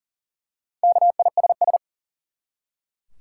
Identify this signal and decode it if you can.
Morse code, heard as a keyed tone, about 40 wpm, 700 Hz. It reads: KISS
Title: Four Letter Words - 40wpm